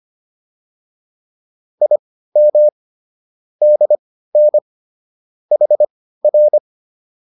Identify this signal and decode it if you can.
Morse code, heard as a keyed tone, about 25 wpm, spaced as for 15 wpm, 600 Hz. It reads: IM DN HR